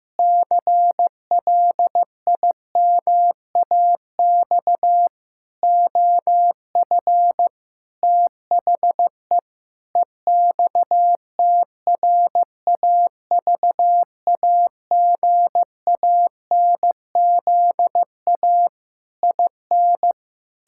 Morse code, heard as a keyed tone, about 16 wpm, 700 Hz. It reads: CLIMAX OF THE EXTRAVAGANZA IN
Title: Pangram Challenge v2 - 15wpm